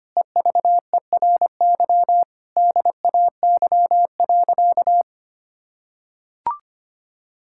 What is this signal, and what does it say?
Morse code, heard as a keyed tone, about 25 wpm, 700 Hz. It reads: EVERY DAY.
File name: practice / 25wpm / Difficult Sentences with Hints - 25wpm